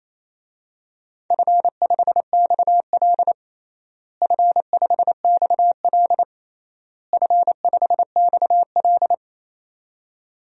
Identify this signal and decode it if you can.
Morse code, heard as a keyed tone, about 28 wpm, 700 Hz. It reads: F5XL F5XL F5XL
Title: DXCC-100 Call Signs - 4 Characters - Repeated 3x - 28wpm